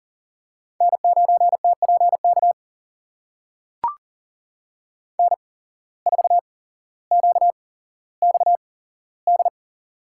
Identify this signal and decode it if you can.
Morse code, heard as a keyed tone, about 40 wpm, spaced as for 17 wpm, 700 Hz. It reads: N9TPK N4QXD
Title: USA Call Signs - 5 Characters - 8x Character Spacing - 40wpm